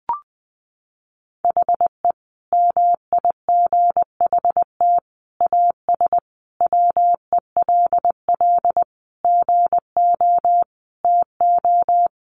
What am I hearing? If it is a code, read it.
Morse code, heard as a keyed tone, about 20 wpm, 700 Hz. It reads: HE MIGHT AS WELL GO TO
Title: Sentences from Top 1300 Words 20wpm